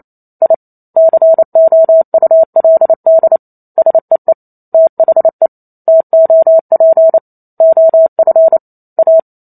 Morse code, heard as a keyed tone, about 28 wpm, 650 Hz. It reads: I COULD SEE THE TOP OF A